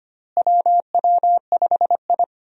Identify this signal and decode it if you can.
Morse code, heard as a keyed tone, about 25 wpm, 700 Hz. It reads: WW5I
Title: USA Call Signs - 4 Characters - Repeated 3x - 25wpm